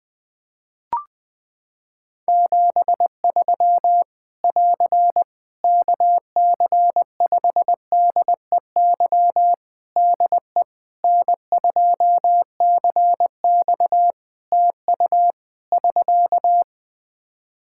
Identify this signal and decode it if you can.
Morse code, heard as a keyed tone, about 21 wpm, 700 Hz. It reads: 73 <AR> KC5DEY DE N2CX TU <SK>
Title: Practice QSO - Exchange 4 - Ending - 20wpm